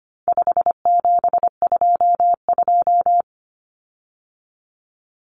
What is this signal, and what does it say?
Morse code, heard as a keyed tone, about 25 wpm, 700 Hz. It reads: 5722
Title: Common Cut Numbers - 4-Digit 25wpm